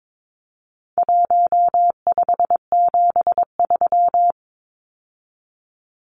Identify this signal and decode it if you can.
Morse code, heard as a keyed tone, about 22 wpm, 700 Hz. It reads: 1573
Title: Numbers - Contest Style - 4-Digit 22wpm